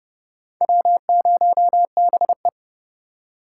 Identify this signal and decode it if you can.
Morse code, heard as a keyed tone, about 30 wpm, 700 Hz. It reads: W0BE
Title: USA Call Signs - 4 Characters - 3x Character Spacing - 30wpm